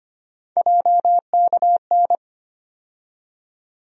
Morse code, heard as a keyed tone, about 25 wpm, 700 Hz. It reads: JKN
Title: USA Call Sign Postfixes - 3 Letters 25wpm